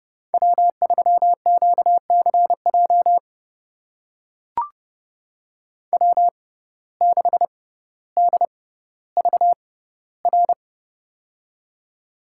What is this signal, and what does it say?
Morse code, heard as a keyed tone, about 32 wpm, spaced as for 16 wpm, 700 Hz. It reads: W3QCJ W6DVR
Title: USA Call Signs - 5 Characters - 6x Character Spacing - 30wpm